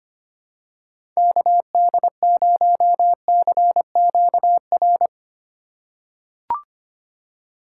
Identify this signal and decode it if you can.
Morse code, heard as a keyed tone, about 25 wpm, 690 Hz. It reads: KD0CQR
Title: USA Call Signs - 6 Characters - 2x Character Spacing - 25wpm